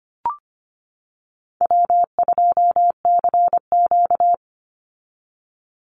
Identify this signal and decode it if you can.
Morse code, heard as a keyed tone, about 25 wpm, 700 Hz. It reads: W2CQ
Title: Call Signs - 25wpm